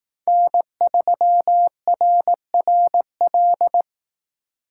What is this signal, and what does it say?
Morse code, heard as a keyed tone, about 18 wpm, 700 Hz. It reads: N3RRL
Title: Speed Racing - USA 1x3 Call Signs 15wpm